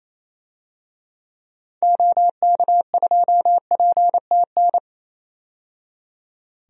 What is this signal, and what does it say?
Morse code, heard as a keyed tone, about 28 wpm, 700 Hz. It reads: OK2PTN
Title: DXCC-100 Call Signs - 6 Characters - 28wpm